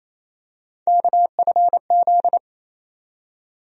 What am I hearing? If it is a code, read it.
Morse code, heard as a keyed tone, about 28 wpm, 700 Hz. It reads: KFZ
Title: USA Call Sign Postfixes - 3 Letters 28wpm